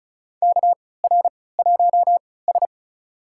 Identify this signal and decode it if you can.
Morse code, heard as a keyed tone, about 36 wpm, spaced as for 20 wpm, 700 Hz. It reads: KR1S
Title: USA Call Signs - 4 Characters - 3x Character Spacing - 35wpm